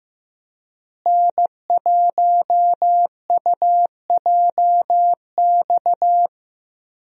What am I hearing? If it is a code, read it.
Morse code, heard as a keyed tone, about 15 wpm, 700 Hz. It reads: N1UJX
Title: USA Call Signs - 5 Characters - Repeated 2x - 15wpm